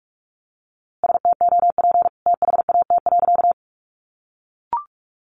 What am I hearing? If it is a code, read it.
Morse code, heard as a keyed tone, about 44 wpm, 700 Hz. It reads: STOP THAT.
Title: A Sound Of Thunder 45wpm